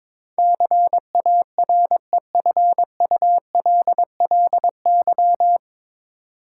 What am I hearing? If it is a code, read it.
Morse code, heard as a keyed tone, about 22 wpm, 700 Hz. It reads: CAREFULLY
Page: Morse Code Ninja practice 22wpm